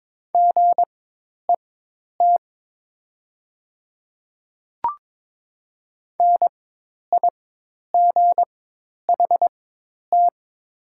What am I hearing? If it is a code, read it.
Morse code, heard as a keyed tone, about 22 wpm, spaced as for 10 wpm, 700 Hz. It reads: GET NIGHT
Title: 4x Character Spacing - Top 300 Words - 22wpm